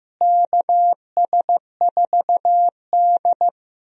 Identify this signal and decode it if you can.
Morse code, heard as a keyed tone, about 15 wpm, 700 Hz. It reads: KS4D
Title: USA Call Signs - 4 Characters - Repeated 3x - 15wpm